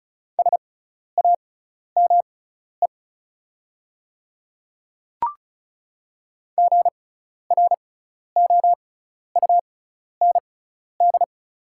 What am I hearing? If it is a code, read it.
Morse code, heard as a keyed tone, about 36 wpm, spaced as for 12 wpm, 690 Hz. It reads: SAME GROUND
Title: 6x Character Spacing - Top 500 Words - 35wpm